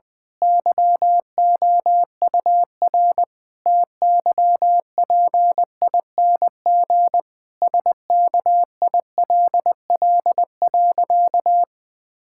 Morse code, heard as a keyed tone, about 20 wpm, 700 Hz. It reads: YOUR TYPING SKILL.
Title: Pangram Challenge v2 - 20wpm